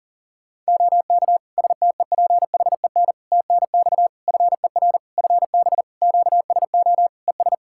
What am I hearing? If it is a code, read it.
Morse code, heard as a keyed tone, about 40 wpm, 700 Hz. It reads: OK STEPHEN TNX FER FB QSO ES
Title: Practice QSO - Exchange 4 - Ending - 40wpm